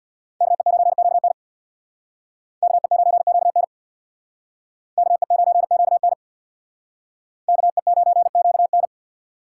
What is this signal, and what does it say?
Morse code, heard as a keyed tone, about 62 wpm, 700 Hz. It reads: KE9QN KE9QN KE9QN KE9QN
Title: Speed Racing - USA Call Signs 50wpm